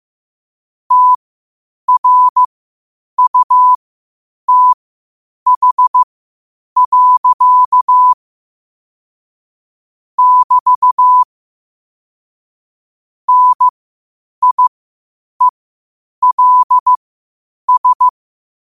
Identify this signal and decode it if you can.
Morse code, heard as a keyed tone, about 15 wpm, spaced as for 8 wpm, 1000 Hz. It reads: TRUTH. <BT> NIELS
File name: qotdfiles / 8wpm